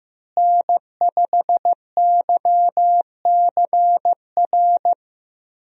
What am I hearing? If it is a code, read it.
Morse code, heard as a keyed tone, about 15 wpm, 700 Hz. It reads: N5YCR
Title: USA Call Signs - 5 Characters - Repeated 2x - 15wpm